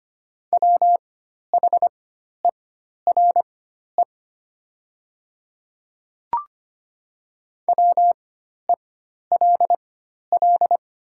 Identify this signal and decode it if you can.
Morse code, heard as a keyed tone, about 25 wpm, spaced as for 12 wpm, 700 Hz. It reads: WHERE WELL
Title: 4x Character Spacing - Top 200 Words - 25wpm